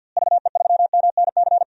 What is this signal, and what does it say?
Morse code, heard as a keyed tone, about 52 wpm, 700 Hz. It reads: VE3MNC
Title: DXCC-100 Call Signs - 6 Characters - Repeated 3x - 50wpm